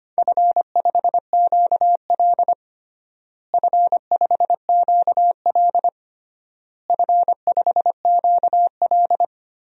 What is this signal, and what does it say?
Morse code, heard as a keyed tone, about 25 wpm, 700 Hz. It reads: F5QL F5QL F5QL
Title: DXCC-100 Call Signs - 4 Characters - Repeated 3x - 25wpm